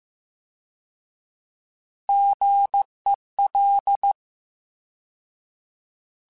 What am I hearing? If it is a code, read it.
Morse code, heard as a keyed tone, about 15 wpm, 800 Hz. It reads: GEL